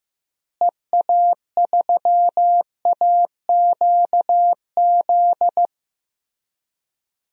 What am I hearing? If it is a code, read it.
Morse code, heard as a keyed tone, about 15 wpm, 700 Hz. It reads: EA3AQZ
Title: DXCC-100 Call Signs - 6 Characters - Repeated 2x - 15wpm